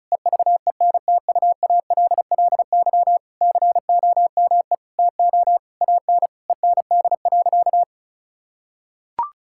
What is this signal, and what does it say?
Morse code, heard as a keyed tone, about 34 wpm, 700 Hz. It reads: EVENTUALLY COME TO AN END.
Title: Sentences from Top 1000 Words - Review Part 6 - 35wpm